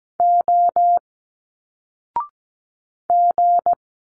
Morse code, heard as a keyed tone, about 17 wpm, 700 Hz. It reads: O G
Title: Single Letters 17wpm